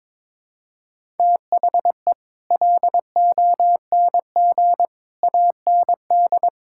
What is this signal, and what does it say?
Morse code, heard as a keyed tone, about 21 wpm, 700 Hz. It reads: THE LONG AND
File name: practice / 22wpm / Sets of 8 Words - 22wpm